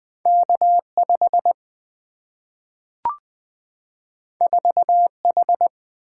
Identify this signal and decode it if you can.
Morse code, heard as a keyed tone, about 20 wpm, 700 Hz. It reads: K5 4H
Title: Single-Letter Single-Number 20wpm